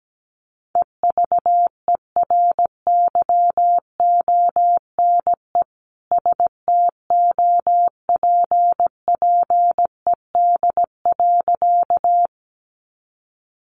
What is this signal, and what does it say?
Morse code, heard as a keyed tone, about 17 wpm, 700 Hz. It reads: EVERYONE STOPPED.
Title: A Sound Of Thunder 17wpm